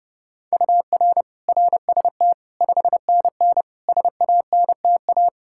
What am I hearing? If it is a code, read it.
Morse code, heard as a keyed tone, about 32 wpm, 700 Hz. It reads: UR RST 5NN SANTA
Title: Practice QSO - Exchange 1 - Beginning - 30wpm